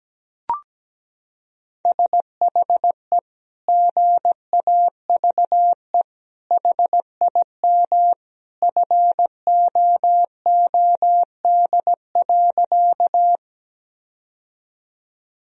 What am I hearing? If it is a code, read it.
Morse code, heard as a keyed tone, about 17 wpm, 700 Hz. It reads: SHE GAVE HIM FOOD.
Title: Sentences from Top 1000 Words - Review Part 1 - 17wpm